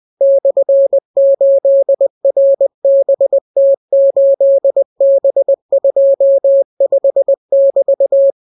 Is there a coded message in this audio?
Yes: Morse code from 0.2 to 8.3 s, heard as a keyed tone, about 20 wpm, 550 Hz. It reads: /8RBT8B25<BT>